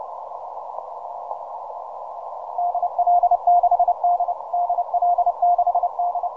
Morse code, heard as a keyed tone, about 31 wpm, 700 Hz. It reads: DL6D DL6D
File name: simple-file-list / VP8SGI 12MCW